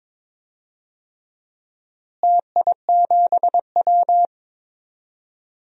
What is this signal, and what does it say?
Morse code, heard as a keyed tone, about 22 wpm, 700 Hz. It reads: TI7W
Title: DXCC-100 Call Signs - 4 Characters - Repeated 3x - 22wpm